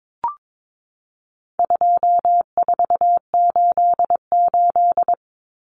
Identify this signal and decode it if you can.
Morse code, heard as a keyed tone, about 22 wpm, 700 Hz. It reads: 2488
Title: Numbers - 4-Digit 22wpm